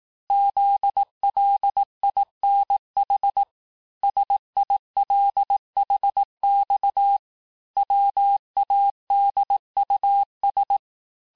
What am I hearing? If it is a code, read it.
Morse code, heard as a keyed tone, about 18 wpm, 800 Hz. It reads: ZLINH SILHX WADUS
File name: cours-du-lundi files / cours-1588-texte-chiffre-vitesse-17MPM